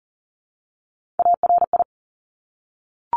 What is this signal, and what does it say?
Morse code, heard as a keyed tone, about 38 wpm, 700 Hz. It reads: ARI
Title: Top Three Letter Combinations in Words 40wpm